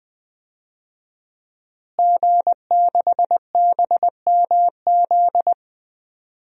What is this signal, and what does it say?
Morse code, heard as a keyed tone, about 20 wpm, 700 Hz. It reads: G6BMZ
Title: DXCC-100 Call Signs - 5 Characters - Repeated 3x - 20wpm